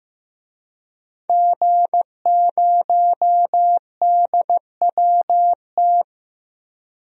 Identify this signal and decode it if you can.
Morse code, heard as a keyed tone, about 16 wpm, 700 Hz. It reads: G0DWT
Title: DXCC-100 Call Signs - 5 Characters - Repeated 2x - 15wpm